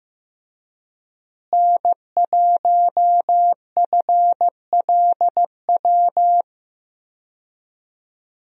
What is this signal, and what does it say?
Morse code, heard as a keyed tone, about 14 wpm, 700 Hz. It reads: N1FLW
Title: USA Call Signs - 5 Characters - Repeated 3x - 15wpm